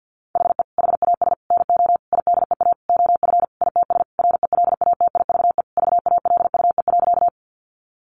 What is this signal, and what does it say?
Morse code, heard as a keyed tone, about 50 wpm, 700 Hz. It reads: HE HAS NO IDEA OF ITS RELATIVE VALUE.
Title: Sentences from Top 1400 Words 50wpm